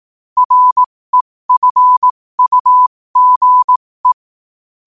Morse code, heard as a keyed tone, about 18 wpm, spaced as for 15 wpm, 1000 Hz. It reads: REFUGE
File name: qotdfiles / 15wpm